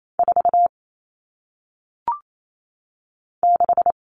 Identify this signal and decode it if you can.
Morse code, heard as a keyed tone, about 28 wpm, 700 Hz. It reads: 4 6
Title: All Cut Numbers - 1-Digit 28wpm